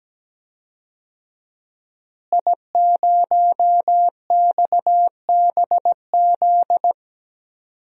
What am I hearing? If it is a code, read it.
Morse code, heard as a keyed tone, about 17 wpm, 700 Hz. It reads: I0XBZ
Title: DXCC-100 Call Signs - 5 Characters - Repeated 2x - 17wpm